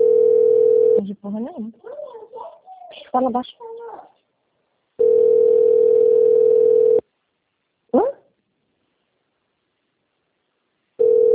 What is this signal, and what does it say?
Ringback tone (North America), 440 + 480 Hz, with speech.
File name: out- - - - - .2889